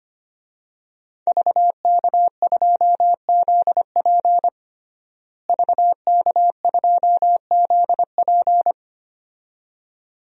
Morse code, heard as a keyed tone, about 25 wpm, 700 Hz. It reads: VK2ZP VK2ZP
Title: DXCC-100 Call Signs - 5 Characters - Repeated 2x - 25wpm